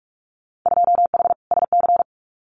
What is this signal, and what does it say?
Morse code, heard as a keyed tone, about 46 wpm, 700 Hz. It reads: 2H SC
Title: Winter Field Day Exchanges 45wpm